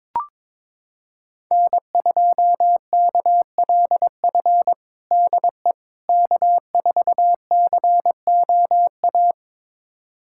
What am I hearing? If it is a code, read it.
Morse code, heard as a keyed tone, about 21 wpm, 700 Hz. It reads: N2KLF DE K4COA
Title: USA Callsign DE Callsign - 5 Characters - 22wpm